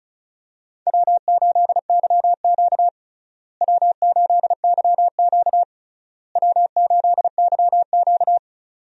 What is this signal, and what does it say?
Morse code, heard as a keyed tone, about 34 wpm, 700 Hz. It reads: W8YQ W8YQ W8YQ
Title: USA Call Signs - 4 Characters - Repeated 3x - 35wpm